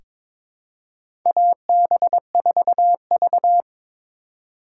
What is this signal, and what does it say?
Morse code, heard as a keyed tone, about 22 wpm, 700 Hz. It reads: AB4V
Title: USA Call Signs - 2x Character Spacing - 22wpm